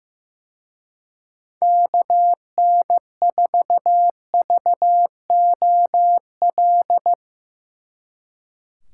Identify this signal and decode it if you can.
Morse code, heard as a keyed tone, about 15 wpm, 700 Hz. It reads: KN4VOL
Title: Speed Racing - USA 2x3 Call Signs 15wpm